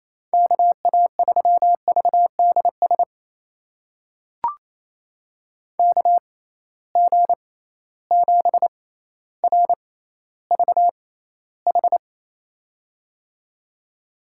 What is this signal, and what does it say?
Morse code, heard as a keyed tone, about 28 wpm, spaced as for 14 wpm, 700 Hz. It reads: KA3VDS KG7RVH
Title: USA Call Signs - 6 Characters - 6x Character Spacing - 28wpm